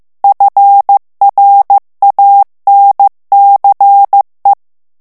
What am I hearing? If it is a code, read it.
Morse code, heard as a keyed tone, about 15 wpm, 800 Hz. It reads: FRANCE
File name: morse mp3 / FRANCAIS TEXTE CLAIR 1 VITESSE 14 MOTS MINUTES